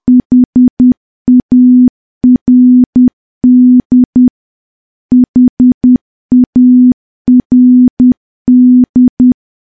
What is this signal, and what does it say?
Morse code, heard as a keyed tone, about 10 wpm, 250 Hz. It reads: HARD HARD
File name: morse / morse-training-10WPM